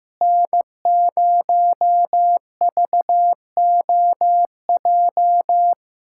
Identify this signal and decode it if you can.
Morse code, heard as a keyed tone, about 15 wpm, 700 Hz. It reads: N0VOJ
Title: USA Callsign DE Callsign - 5 Characters - 15wpm